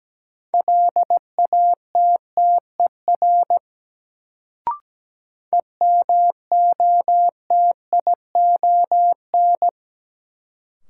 Morse code, heard as a keyed tone, about 17 wpm, 700 Hz. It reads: LATTER EMOTION
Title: Sentences from Top 1600 Words - New Word Drill - 17wpm